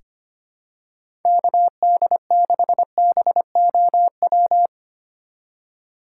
Morse code, heard as a keyed tone, about 25 wpm, 700 Hz. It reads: KD6BOW
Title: USA Call Signs - 6 Characters - 2x Character Spacing - 25wpm